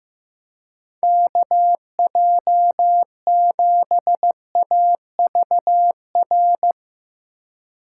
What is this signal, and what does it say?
Morse code, heard as a keyed tone, about 15 wpm, 700 Hz. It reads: KJ7AVR
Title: USA Call Signs - 6 Characters - 4x Character Spacing - 15wpm